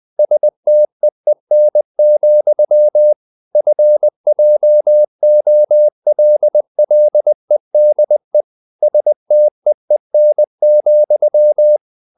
Morse code, heard as a keyed tone, about 20 wpm, 600 Hz. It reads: STEEN, FJOLLEDE STEEN,
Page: Ferie 20 wpm